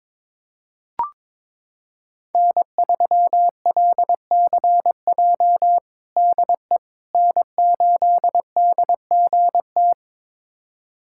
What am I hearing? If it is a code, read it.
Morse code, heard as a keyed tone, about 21 wpm, 700 Hz. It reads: N3LCJ DE N8DGT
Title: USA Callsign DE Callsign 22wpm